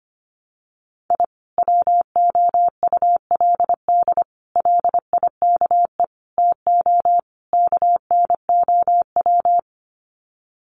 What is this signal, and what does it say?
Morse code, heard as a keyed tone, about 25 wpm, 700 Hz. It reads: I WOULD LIKE TO KNOW
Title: Sets of 5 Words Encore 25wpm